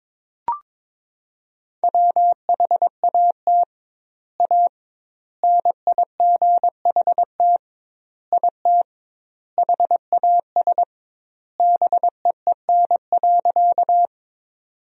Morse code, heard as a keyed tone, about 22 wpm, 700 Hz. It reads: WHAT A NIGHT IT HAS BEEN.
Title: 2x Word-Spacing - Sentences from Top 300 Words - 22wpm